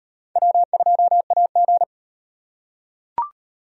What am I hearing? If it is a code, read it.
Morse code, heard as a keyed tone, about 38 wpm, 700 Hz. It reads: W2AG